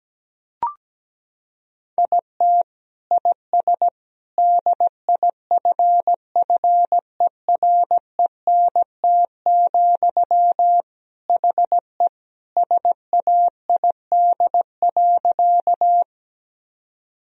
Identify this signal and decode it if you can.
Morse code, heard as a keyed tone, about 17 wpm, 700 Hz. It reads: IT IS DIFFERENT, HE SAID.
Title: Sentences from Top 1000 Words - Review Part 3 - 17wpm